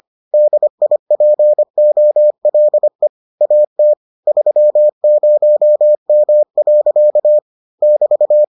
Morse code, heard as a keyed tone, about 25 wpm, 600 Hz. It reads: DIPOLE AT 30M. <BT>